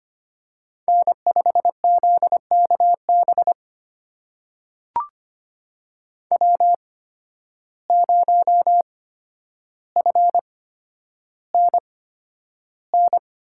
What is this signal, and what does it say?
Morse code, heard as a keyed tone, about 25 wpm, spaced as for 11 wpm, 700 Hz. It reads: N5ZKB W0FNN
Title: USA Call Signs - 5 Characters - 8x Character Spacing - 25wpm